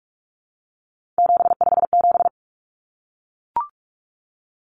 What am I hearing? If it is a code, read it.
Morse code, heard as a keyed tone, about 44 wpm, 700 Hz. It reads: 757
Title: Common Cut Numbers - 3-Digit 45wpm